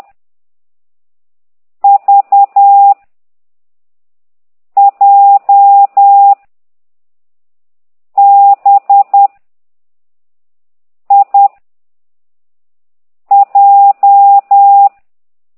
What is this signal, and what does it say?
Morse code, heard as a keyed tone, about 10 wpm, 800 Hz. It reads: V J B I J